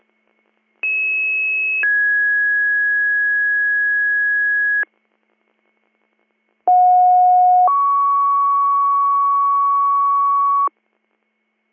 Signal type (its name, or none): two-tone page